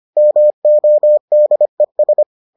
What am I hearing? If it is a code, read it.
Morse code, heard as a keyed tone, about 25 wpm, 600 Hz. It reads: MODES